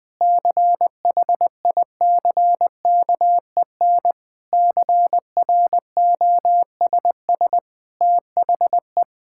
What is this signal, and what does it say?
Morse code, heard as a keyed tone, about 20 wpm, 700 Hz. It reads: CHICKEN CROSS THE
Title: Sentences from Top 1000 Words - Review Part 5 - 20wpm